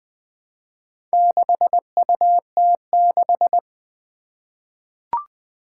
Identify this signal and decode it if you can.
Morse code, heard as a keyed tone, about 20 wpm, 700 Hz. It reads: 6UT6
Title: Four Challenging Letter-Number Combinations 20wpm